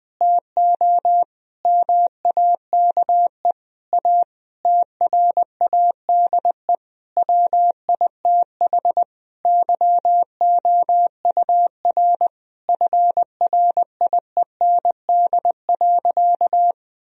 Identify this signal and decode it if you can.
Morse code, heard as a keyed tone, about 21 wpm, 700 Hz. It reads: TO MAKE A TRADE WITH YOUR FRIEND.
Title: Sentences from Top 1000 Words - Review Part 4 - 20wpm